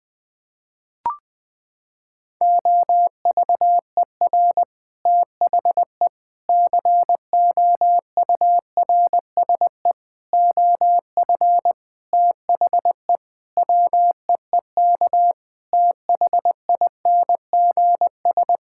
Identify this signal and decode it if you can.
Morse code, heard as a keyed tone, about 20 wpm, 700 Hz. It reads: OVER THE COURSE OF THE WEEK THINGS
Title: Sentences with 4-Word Phrases - 20wpm